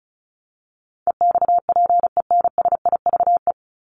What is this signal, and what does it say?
Morse code, heard as a keyed tone, about 34 wpm, 700 Hz. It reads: EXPENSIVE
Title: Top 751-1000 Words 35wpm